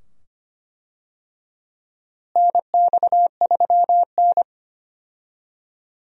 Morse code, heard as a keyed tone, about 25 wpm, 700 Hz. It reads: NX3N
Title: USA Call Signs - 4 Characters - Repeated 2x - 25wpm